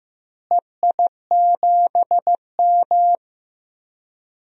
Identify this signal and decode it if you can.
Morse code, heard as a keyed tone, about 15 wpm, 700 Hz. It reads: EI7M
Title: DXCC-100 Call Signs - 4 Characters - Repeated 2x - 15wpm